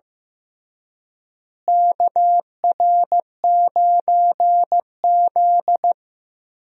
Morse code, heard as a keyed tone, about 15 wpm, 700 Hz. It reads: KR9Z
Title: USA Call Signs - 4 Characters - Repeated 3x - 15wpm